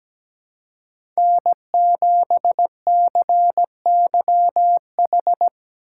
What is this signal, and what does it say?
Morse code, heard as a keyed tone, about 17 wpm, 700 Hz. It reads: N7CYH
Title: USA Call Signs - 4x Character Spacing - 17wpm